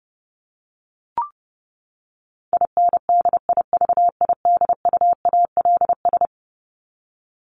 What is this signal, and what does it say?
Morse code, heard as a keyed tone, about 30 wpm, 700 Hz. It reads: INDIVIDUALS
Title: ES:S Suffix - Top 100 Words with Base Words 30wpm